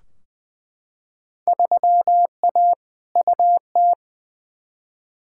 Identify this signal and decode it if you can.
Morse code, heard as a keyed tone, about 20 wpm, 700 Hz. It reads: 3A UT
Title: Field Day Exchanges - 20wpm